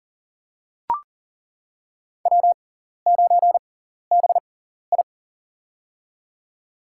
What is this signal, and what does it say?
Morse code, heard as a keyed tone, about 40 wpm, spaced as for 14 wpm, 700 Hz. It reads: W9BI
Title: USA Call Signs - 4 Characters - 6x Character Spacing - 40wpm